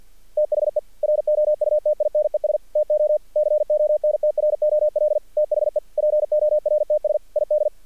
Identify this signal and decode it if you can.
Morse code, heard as a keyed tone, about 48 wpm, 600 Hz. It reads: THE ROUTINES TO CONTROL THE PORTS ID
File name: archived / CRSnet